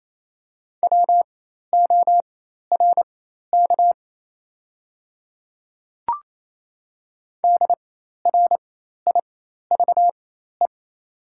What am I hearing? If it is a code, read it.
Morse code, heard as a keyed tone, about 28 wpm, spaced as for 13 wpm, 700 Hz. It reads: WORK DRIVE
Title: 4x Character Spacing - Top 500 Words - 28wpm